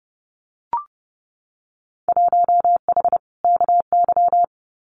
Morse code, heard as a keyed tone, about 30 wpm, 700 Hz. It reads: 1H KY
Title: Winter Field Day Exchanges 30wpm